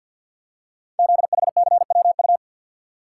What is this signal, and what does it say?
Morse code, heard as a keyed tone, about 52 wpm, 700 Hz. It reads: ZHCWU
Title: Code Groups - 5 Letters - 50wpm